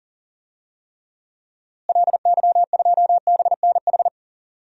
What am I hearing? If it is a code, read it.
Morse code, heard as a keyed tone, about 40 wpm, 700 Hz. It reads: LY2BNH
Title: DXCC-100 Call Signs - 6 Characters - Repeated 3x - 40wpm